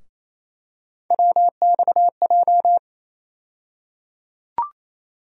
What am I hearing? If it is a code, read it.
Morse code, heard as a keyed tone, about 28 wpm, 700 Hz. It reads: WXJ
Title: USA Call Sign Postfixes - 3 Letters 28wpm